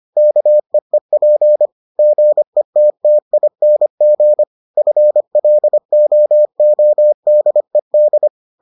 Morse code, heard as a keyed tone, about 25 wpm, 600 Hz. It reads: KEEP GETTING FLOODED